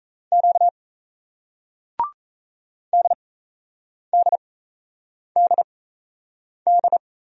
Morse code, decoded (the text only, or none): Q DDDD